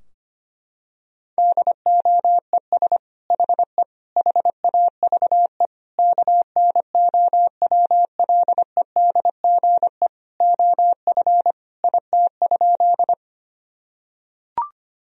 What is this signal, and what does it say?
Morse code, heard as a keyed tone, about 25 wpm, 700 Hz. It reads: DOES HE HAVE KNOWLEDGE OF IT?
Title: Sentences from Top 1000 Words - Review Part 3 - 25wpm